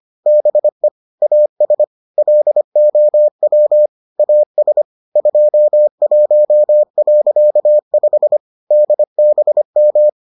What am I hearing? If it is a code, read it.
Morse code, heard as a keyed tone, about 25 wpm, 600 Hz. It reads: BE AS LOW AS 21.5 DBM